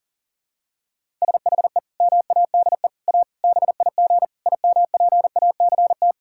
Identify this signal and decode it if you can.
Morse code, heard as a keyed tone, about 40 wpm, 690 Hz. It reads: SHE MADE A BIG IMPACT
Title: Sentences from Top 1000 Words - Review Part 6 - 40wpm